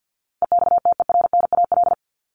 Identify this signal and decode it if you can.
Morse code, heard as a keyed tone, about 50 wpm, 700 Hz. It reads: EXTERNAL
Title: Top 1751-2000 Words 50wpm